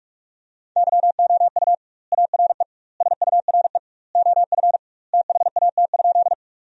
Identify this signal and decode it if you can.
Morse code, heard as a keyed tone, about 46 wpm, 700 Hz. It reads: YOU ARE SURE OF THAT?
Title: 2x Word-Spacing - Sentences from Top 300 Words - 45wpm